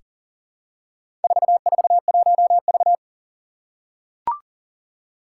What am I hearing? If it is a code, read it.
Morse code, heard as a keyed tone, about 41 wpm, 700 Hz. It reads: 441V
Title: Four Challenging Letter-Number Combinations 40wpm